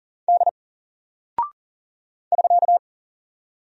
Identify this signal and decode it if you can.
Morse code, heard as a keyed tone, about 42 wpm, 700 Hz. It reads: D <SK>
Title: Single Character (Letter:Number:Punctuation:Prosign) 40wpm